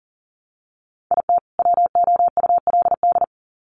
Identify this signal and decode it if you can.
Morse code, heard as a keyed tone, about 40 wpm, 700 Hz. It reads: IT WOULD
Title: Sets of 2 Words 40wpm